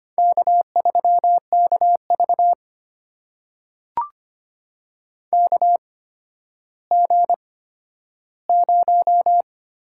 Morse code, heard as a keyed tone, about 25 wpm, spaced as for 12 wpm, 700 Hz. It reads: K3KV KG0
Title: USA Call Signs - 8x Character Spacing - 25wpm